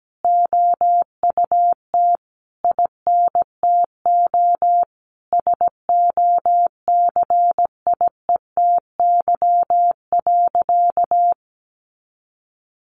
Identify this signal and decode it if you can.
Morse code, heard as a keyed tone, about 17 wpm, 700 Hz. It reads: OUT INTO SOCIETY.
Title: Sentences from Top 700 Words 17wpm